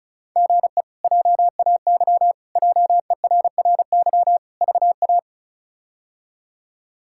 Morse code, heard as a keyed tone, about 36 wpm, 700 Hz. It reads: GE JAY JERRY VA
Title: K1USN SST QSO Practice - 35wpm